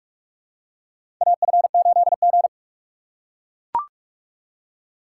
Morse code, heard as a keyed tone, about 44 wpm, 700 Hz. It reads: AF8G
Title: USA Call Signs - 4 Characters - 6x Character Spacing - 45wpm